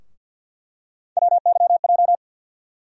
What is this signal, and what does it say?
Morse code, heard as a keyed tone, about 50 wpm, 700 Hz. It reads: WYJ
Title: USA Call Sign Postfixes - 3 Letters 50wpm